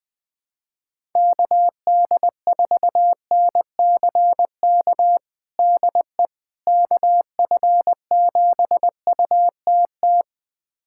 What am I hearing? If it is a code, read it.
Morse code, heard as a keyed tone, about 20 wpm, 700 Hz. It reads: KD4NCK DE KF7UTT
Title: USA Callsign DE Callsign 20wpm